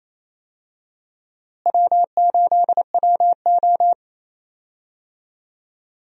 Morse code, heard as a keyed tone, about 28 wpm, 700 Hz. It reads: W8WO
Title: USA Call Signs - 4 Characters - Repeated 2x - 28wpm